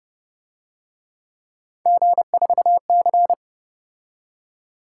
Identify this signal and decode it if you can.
Morse code, heard as a keyed tone, about 29 wpm, 700 Hz. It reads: G4C